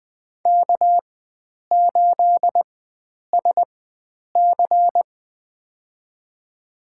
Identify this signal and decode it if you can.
Morse code, heard as a keyed tone, about 20 wpm, spaced as for 9 wpm, 700 Hz. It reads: K8SC
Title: USA Call Signs - 4 Characters - 4x Character Spacing - 20wpm